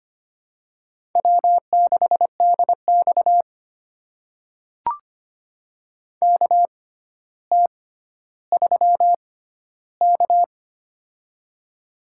Morse code, heard as a keyed tone, about 25 wpm, spaced as for 13 wpm, 700 Hz. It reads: W6DX KT3K
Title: USA Call Signs - 4 Characters - 6x Character Spacing - 25wpm